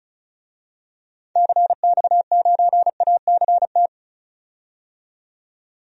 Morse code, heard as a keyed tone, about 34 wpm, 700 Hz. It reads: CX9ACT
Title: DXCC-100 Call Signs - 6 Characters - 35wpm